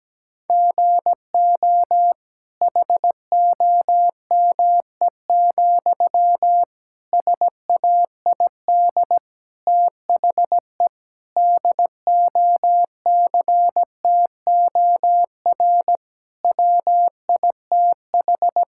Morse code, heard as a keyed tone, about 17 wpm, 700 Hz. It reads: GO HOME, SAID THE DOCTOR WITH
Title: Sentences from Top 1600 Words 17wpm